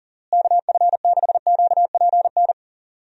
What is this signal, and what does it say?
Morse code, heard as a keyed tone, about 40 wpm, 700 Hz. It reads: KF6QPN
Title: USA Call Signs - 6 Characters - Repeated 2x - 40wpm